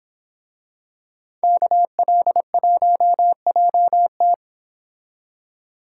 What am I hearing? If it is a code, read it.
Morse code, heard as a keyed tone, about 26 wpm, 700 Hz. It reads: KL1JT
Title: Speed Racing - USA 2x2 Call Signs 22wpm